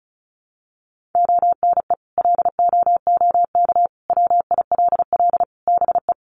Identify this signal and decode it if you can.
Morse code, heard as a keyed tone, about 34 wpm, 700 Hz. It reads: ONE LOOK WILL BE
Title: Sentences from Top 1400 Words 35wpm